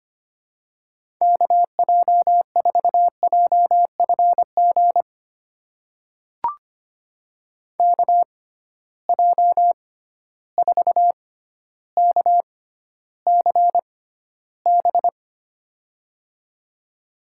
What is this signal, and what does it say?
Morse code, heard as a keyed tone, about 25 wpm, spaced as for 13 wpm, 700 Hz. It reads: KJ4JFG KJ4KCB
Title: USA Call Signs - 6 Characters - 6x Character Spacing - 25wpm